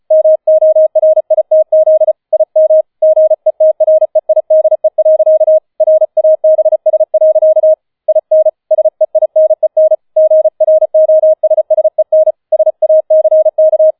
Morse code, heard as a keyed tone, about 35 wpm, 630 Hz. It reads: MORITZ IM GETREIDE. RABS. IN SEINEN GROSSEN SACK